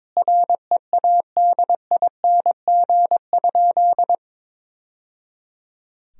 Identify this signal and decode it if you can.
Morse code, heard as a keyed tone, about 22 wpm, 700 Hz. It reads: READING?
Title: Sentences from Top 1000 Words - Review Part 3 - 22wpm